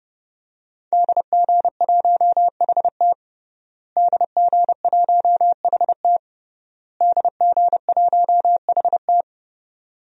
Morse code, heard as a keyed tone, about 32 wpm, 700 Hz. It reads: DG1HT DG1HT DG1HT
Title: DXCC-100 Call Signs - 5 Characters - Repeated 3x - 30wpm